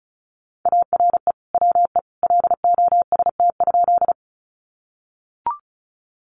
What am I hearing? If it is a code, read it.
Morse code, heard as a keyed tone, about 34 wpm, 700 Hz. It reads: ARE WE LOST?
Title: Sentences from Top 1400 Words 35wpm